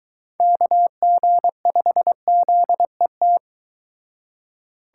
Morse code, heard as a keyed tone, about 22 wpm, 700 Hz. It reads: KG5ZET